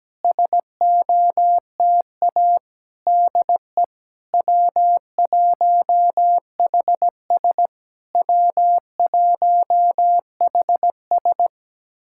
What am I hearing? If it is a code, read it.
Morse code, heard as a keyed tone, about 17 wpm, 690 Hz. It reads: SOTA DE W1HS W1HS
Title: SOTA QSO Practice 17wpm